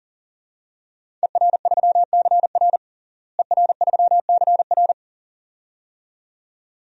Morse code, heard as a keyed tone, about 40 wpm, 700 Hz. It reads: ER3CR ER3CR
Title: DXCC-100 Call Signs - 5 Characters - Repeated 2x - 40wpm